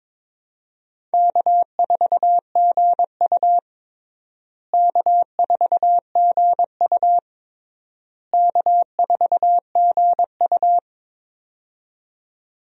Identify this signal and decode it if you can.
Morse code, heard as a keyed tone, about 22 wpm, 700 Hz. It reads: K4GU K4GU K4GU
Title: USA Call Signs - 4 Characters - Repeated 3x - 22wpm